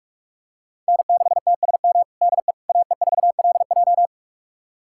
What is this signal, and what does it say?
Morse code, heard as a keyed tone, about 47 wpm, 700 Hz. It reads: N6TSM DE AE4LJ
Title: USA Callsign DE Callsign - 5 Characters - 45wpm